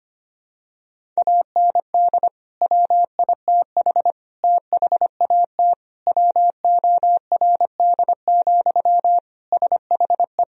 Morse code, heard as a keyed tone, about 25 wpm, 700 Hz. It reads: AND WITH THAT WORD, SHE
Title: Sentences from Top 1000 Words - Review Part 1 - 25wpm